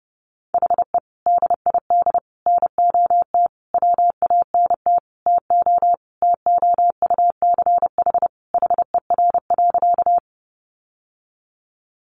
Morse code, heard as a keyed tone, about 30 wpm, 700 Hz. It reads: HE DID NOT WANT TO TOUCH HER.
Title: Sentences from Top 1200 Words 30wpm